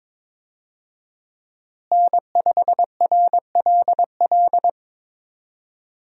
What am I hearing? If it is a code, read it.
Morse code, heard as a keyed tone, about 22 wpm, 700 Hz. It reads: N5RLL
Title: USA Call Signs - 5 Characters - Repeated 2x - 22wpm